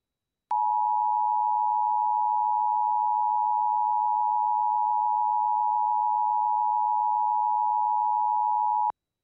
Emergency Alert System (SAME) tones (the 853 and 960 Hz attention tone). It is heard from 0.5 to 9.0 s.